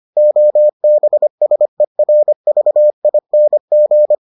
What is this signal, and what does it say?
Morse code, heard as a keyed tone, about 25 wpm, 600 Hz. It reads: OBSERVING